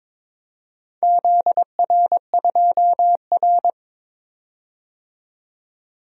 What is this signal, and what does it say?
Morse code, heard as a keyed tone, about 22 wpm, 700 Hz. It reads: ZR2R
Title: DXCC-100 Call Signs - 4 Characters - 22wpm